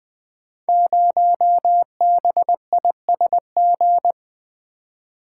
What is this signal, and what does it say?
Morse code, heard as a keyed tone, about 20 wpm, 700 Hz. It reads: 0BISG
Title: Code Groups - 5 Letter:Numbers - 20wpm